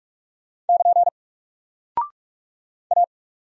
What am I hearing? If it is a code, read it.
Morse code, heard as a keyed tone, about 44 wpm, 700 Hz. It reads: <KN> A
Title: Single Character (Letter:Number:Punctuation:Prosign) Alt 45wpm